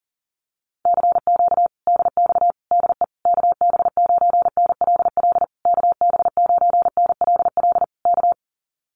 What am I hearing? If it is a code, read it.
Morse code, heard as a keyed tone, about 40 wpm, 700 Hz. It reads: CQ DX DE KB9NLL KB9NLL K
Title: Common QSO Phrases Encore 40wpm